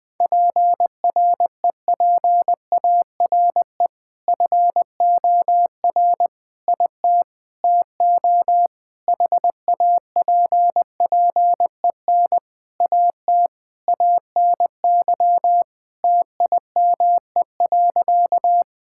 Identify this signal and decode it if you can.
Morse code, heard as a keyed tone, about 20 wpm, 700 Hz. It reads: PREPARE FOR IT TO HAPPEN AT ANY TIME.
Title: Sentences from Top 1000 Words - Review Part 6 - 20wpm